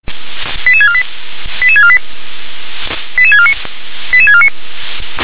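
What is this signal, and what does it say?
Five-tone selective call (ZVEI), heard as quick runs of five tones, tones of about 70 ms. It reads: 88648 88648 88649 88649
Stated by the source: ZVEI five-tone call 88500 received via SDR.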